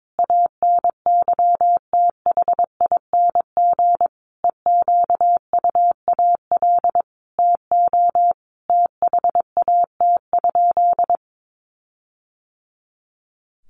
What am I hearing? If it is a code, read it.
Morse code, heard as a keyed tone, about 22 wpm, 700 Hz. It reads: ANYTHING EQUAL TO THAT?
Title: Sentences from Top 1400 Words 22wpm